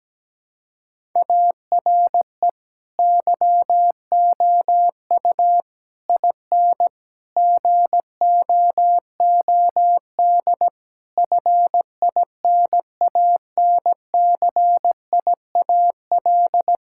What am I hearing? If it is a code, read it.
Morse code, heard as a keyed tone, about 17 wpm, 700 Hz. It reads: ARE YOU IN GOOD FINANCIAL
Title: Sentences from Top 1000 Words - Review Part 5 - 17wpm